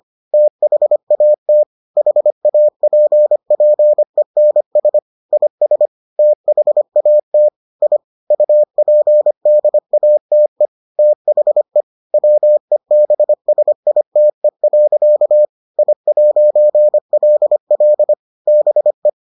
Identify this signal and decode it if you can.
Morse code, heard as a keyed tone, about 25 wpm, 600 Hz. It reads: THAT HAPPENS IS THAT I UPDATE THE WEBSITE. I'LL BE